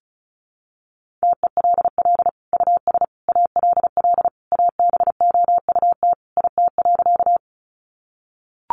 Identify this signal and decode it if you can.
Morse code, heard as a keyed tone, about 34 wpm, 700 Hz. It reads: TELL US ALL ABOUT IT.
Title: Sentences from Top 200 Words 35wpm